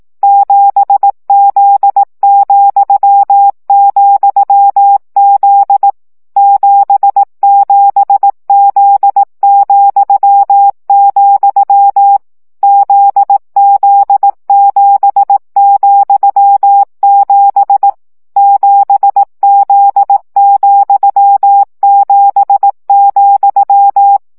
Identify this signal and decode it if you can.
Morse code, heard as a keyed tone, about 18 wpm, 800 Hz. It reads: 7Z,,Z 77Z,, ZZ7,7 7Z,7,
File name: CW gyakorlas z 7 vesszo 15 baud